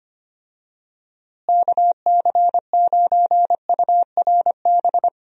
Morse code, heard as a keyed tone, about 25 wpm, 700 Hz. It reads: KC9URB
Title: USA Call Signs - 6 Characters - Repeated 2x - 25wpm